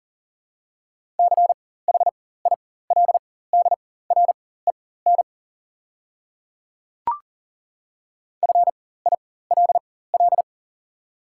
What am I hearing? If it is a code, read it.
Morse code, heard as a keyed tone, about 40 wpm, spaced as for 19 wpm, 700 Hz. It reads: CHILDREN FILL
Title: 4x Character Spacing - Top 500 Words - 40wpm